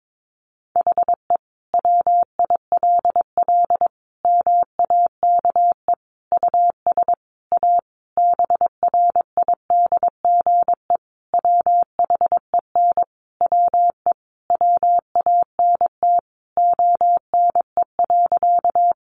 Morse code, heard as a keyed tone, about 22 wpm, 700 Hz. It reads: HE WILL MAKE US A BRIDGE WHEN WE WANT ONE.
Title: Sentences from Top 1200 Words 22wpm